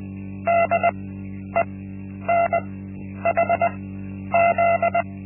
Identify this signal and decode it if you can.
Morse code, heard as a keyed tone, about 20 wpm, spaced as for 10 wpm, 680 Hz. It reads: DENHZ
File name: Linksmakalnis-1